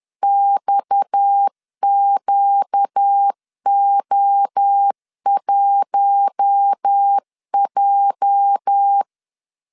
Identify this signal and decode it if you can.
Morse code, heard as a keyed tone, about 10 wpm, 800 Hz. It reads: XQO1J
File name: LEZIONI / 8 YJ45 / 8 YJ 10wpm 2651